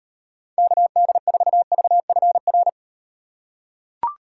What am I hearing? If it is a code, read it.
Morse code, heard as a keyed tone, about 38 wpm, 700 Hz. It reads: KD4VFR